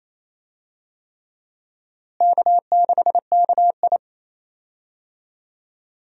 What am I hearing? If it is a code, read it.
Morse code, heard as a keyed tone, about 28 wpm, 700 Hz. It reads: K6KI
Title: USA Call Signs - 4 Characters - Repeated 3x - 28wpm